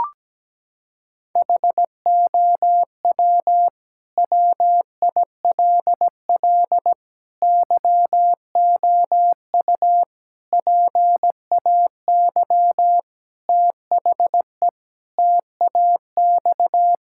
Morse code, heard as a keyed tone, about 17 wpm, 700 Hz. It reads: HOW WILL YOU PAY THE TAX
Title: Sentences from Top 1000 Words - Review Part 5 - 17wpm